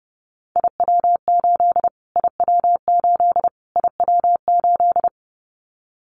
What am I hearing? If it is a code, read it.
Morse code, heard as a keyed tone, about 30 wpm, 700 Hz. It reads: IW8 IW8 IW8
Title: Callsign Prefixes for Top 25 DXCC - Repeated 3x - 30wpm